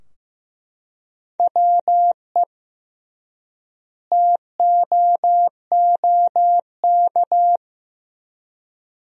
Morse code, heard as a keyed tone, about 15 wpm, 700 Hz. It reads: WE TOOK
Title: 3x Word-Spacing - Sentences from Top 400 Words - 15wpm